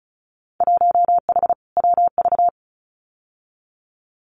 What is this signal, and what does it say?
Morse code, heard as a keyed tone, about 36 wpm, 700 Hz. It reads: 1H WV
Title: Winter Field Day Exchanges 35wpm